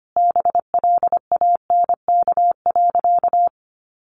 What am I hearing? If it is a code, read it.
Morse code, heard as a keyed tone, about 25 wpm, 700 Hz. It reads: BLANK.
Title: Sentences from Top 1500 Words 25wpm